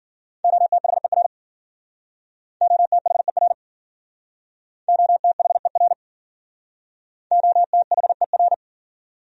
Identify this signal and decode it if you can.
Morse code, heard as a keyed tone, about 50 wpm, 700 Hz. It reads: OTHER OTHER OTHER OTHER